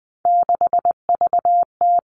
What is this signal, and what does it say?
Morse code, heard as a keyed tone, about 19 wpm, 700 Hz. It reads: 6VT
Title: Three Challenging Letter-Number Combinations 20wpm